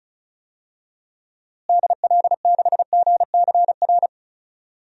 Morse code, heard as a keyed tone, about 34 wpm, 700 Hz. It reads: DL6GCR
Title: DXCC-100 Call Signs - 6 Characters - Repeated 3x - 35wpm